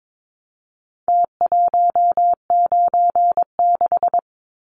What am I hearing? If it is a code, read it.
Morse code, heard as a keyed tone, about 22 wpm, 700 Hz. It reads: T196
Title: Numbers - Contest Style - 4-Digit 22wpm